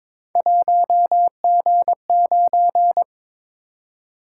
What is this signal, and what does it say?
Morse code, heard as a keyed tone, about 22 wpm, 700 Hz. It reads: 1G9
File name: practice / 22wpm / Three Letter-Number Combinations - 22wpm